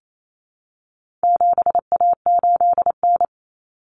Morse code, heard as a keyed tone, about 28 wpm, 700 Hz. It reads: 7A8N
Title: Common Cut Numbers - 4-Digit 28wpm